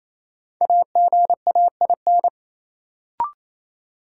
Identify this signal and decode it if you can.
Morse code, heard as a keyed tone, about 28 wpm, 700 Hz. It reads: AGAIN
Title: Sets of 4 Words - New Word Drill - 28wpm